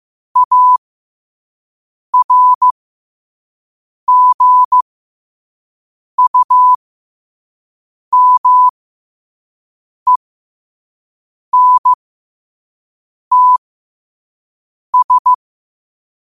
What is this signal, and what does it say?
Morse code, heard as a keyed tone, about 15 wpm, spaced as for 5 wpm, 1000 Hz. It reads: ARGUMENTS
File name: qotdfiles / 5wpm